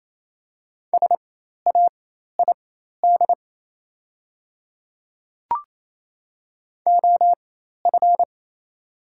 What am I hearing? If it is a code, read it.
Morse code, heard as a keyed tone, about 28 wpm, spaced as for 13 wpm, 700 Hz. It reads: SAID OF
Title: 4x Character Spacing - Top 100 Words - 28wpm